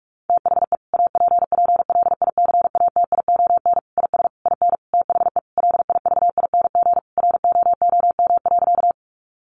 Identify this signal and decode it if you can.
Morse code, heard as a keyed tone, about 44 wpm, 700 Hz. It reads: THE APPLICATION IS IN THE LIVING ROOM.
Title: Sentences from Top 1200 Words 45wpm